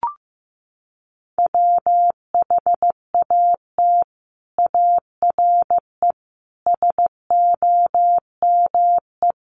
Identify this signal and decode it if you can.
Morse code, heard as a keyed tone, about 15 wpm, 700 Hz. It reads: WHAT ARE SOME
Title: Sets of 5 Words Encore 15wpm